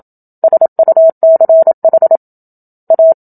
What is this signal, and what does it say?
Morse code, heard as a keyed tone, about 27 wpm, 650 Hz. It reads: SUCH A
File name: Book Two / 01-Chapter 1 - 27 WPM